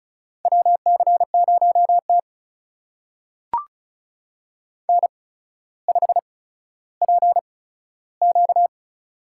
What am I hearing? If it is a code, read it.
Morse code, heard as a keyed tone, about 36 wpm, spaced as for 15 wpm, 700 Hz. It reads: WC0T N5PQ
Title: USA Call Signs - 4 Characters - 8x Character Spacing - 35wpm